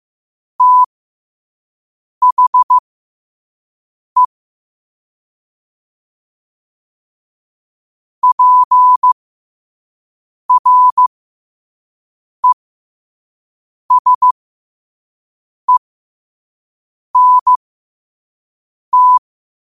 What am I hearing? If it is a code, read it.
Morse code, heard as a keyed tone, about 15 wpm, spaced as for 5 wpm, 1000 Hz. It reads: THE PRESENT
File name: qotdfiles / 5wpm